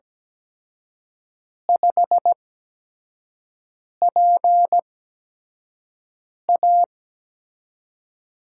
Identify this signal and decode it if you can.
Morse code, heard as a keyed tone, about 17 wpm, spaced as for 5 wpm, 700 Hz. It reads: 5PA
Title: USA Call Signs - 8x Character Spacing - 17wpm